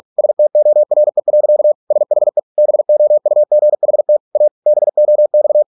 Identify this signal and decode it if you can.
Morse code, heard as a keyed tone, about 46 wpm, 600 Hz. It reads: STORE. SHE BOUGHT A BOX